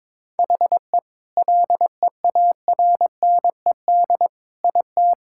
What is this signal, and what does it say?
Morse code, heard as a keyed tone, about 22 wpm, 700 Hz. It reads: HE LEARNED IT
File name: Sentences with 7-Word Phrases - 22wpm